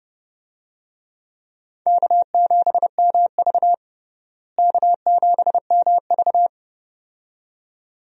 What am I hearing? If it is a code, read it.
Morse code, heard as a keyed tone, about 31 wpm, 700 Hz. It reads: K7MV K7MV
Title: USA Call Signs - 4 Characters - Repeated 2x - 30wpm